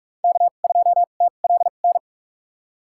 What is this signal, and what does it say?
Morse code, heard as a keyed tone, about 46 wpm, spaced as for 32 wpm, 700 Hz. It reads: K2TLN
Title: USA Call Signs - 2x Character Spacing - 45wpm